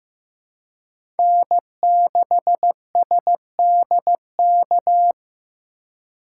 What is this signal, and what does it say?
Morse code, heard as a keyed tone, about 16 wpm, 700 Hz. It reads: N6SDK
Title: USA Call Signs - 2x Character Spacing - 15wpm